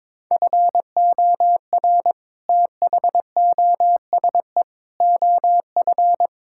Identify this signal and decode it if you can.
Morse code, heard as a keyed tone, about 22 wpm, 700 Hz. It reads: FOR THOSE OF
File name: practice / 22wpm / Sets of 6 Words - 22wpm